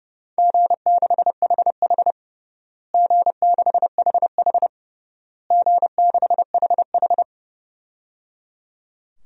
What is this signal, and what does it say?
Morse code, heard as a keyed tone, about 29 wpm, 700 Hz. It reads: G6HH G6HH G6HH
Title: DXCC-100 Call Signs - 4 Characters - Repeated 3x - 30wpm